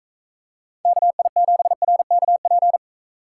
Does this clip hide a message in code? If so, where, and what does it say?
Yes: Morse code from 0.8 to 2.8 s, heard as a keyed tone, about 42 wpm, 700 Hz. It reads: KI7RKP